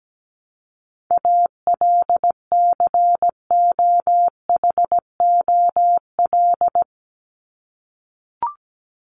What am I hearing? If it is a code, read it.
Morse code, heard as a keyed tone, about 17 wpm, 700 Hz. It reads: ALCOHOL
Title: Top 1251-1500 Words 17wpm